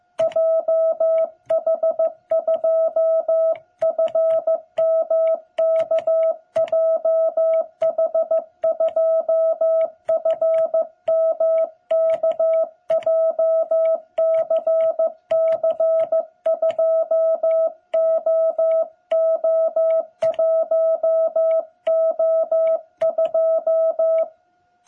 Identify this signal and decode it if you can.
Morse code, heard as a keyed tone, about 15 wpm, 660 Hz. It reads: JH2FMK JH2FMK JCC2OO1O2
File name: jh8cbh / 2021cw